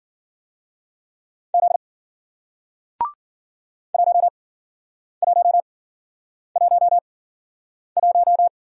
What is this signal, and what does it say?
Morse code, heard as a keyed tone, about 51 wpm, spaced as for 9 wpm, 700 Hz. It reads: Z 1111